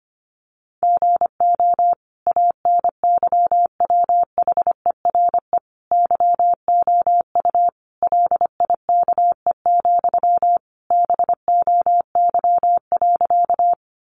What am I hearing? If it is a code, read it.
Morse code, heard as a keyed tone, about 25 wpm, 700 Hz. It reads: GO ANYWHERE YOU LIKE, BOY.
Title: Sentences from Top 1400 Words 25wpm